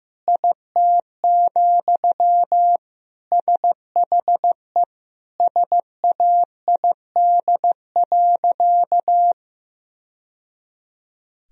Morse code, heard as a keyed tone, about 15 wpm, 700 Hz. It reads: IT, SHE SAID.
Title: Sentences from Top 1000 Words - Review Part 4 - 15wpm